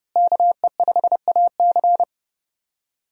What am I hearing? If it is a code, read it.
Morse code, heard as a keyed tone, about 30 wpm, 700 Hz. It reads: KE5AC
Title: USA Call Signs - 5 Characters - Repeated 2x - 30wpm